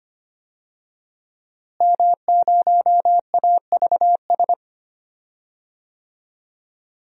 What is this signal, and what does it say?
Morse code, heard as a keyed tone, about 25 wpm, 700 Hz. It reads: M0AVS
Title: DXCC-100 Call Signs - 5 Characters - Repeated 3x - 25wpm